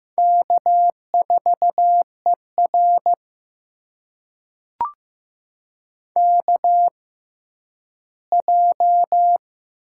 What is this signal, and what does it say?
Morse code, heard as a keyed tone, about 15 wpm, spaced as for 10 wpm, 700 Hz. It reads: K4ER KJ
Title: USA Call Signs - 4 Characters - 6x Character Spacing - 15wpm